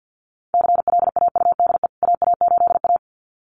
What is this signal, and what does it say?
Morse code, heard as a keyed tone, about 48 wpm, 700 Hz. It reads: CLAUDE AA8A
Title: CWT Exchange 50wpm